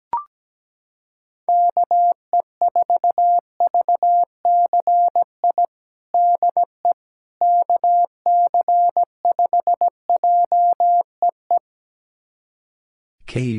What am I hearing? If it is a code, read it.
Morse code, heard as a keyed tone, about 17 wpm, 700 Hz. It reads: KE4VCI DE KC5JEE
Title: USA Callsign DE Callsign - 6 Characters - 17wpm